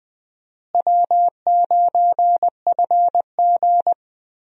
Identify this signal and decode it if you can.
Morse code, heard as a keyed tone, about 20 wpm, 700 Hz. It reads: W9FG
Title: USA Call Signs - 4 Characters - Repeated 2x - 20wpm